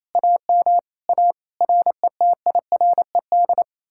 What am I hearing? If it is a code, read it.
Morse code, heard as a keyed tone, about 37 wpm, 700 Hz. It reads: AM A RETIRED